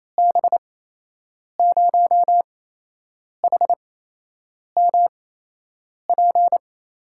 Morse code, heard as a keyed tone, about 28 wpm, spaced as for 8 wpm, 700 Hz. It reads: B0HMP
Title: USA Call Signs - 6 Characters - 8x Character Spacing - 28wpm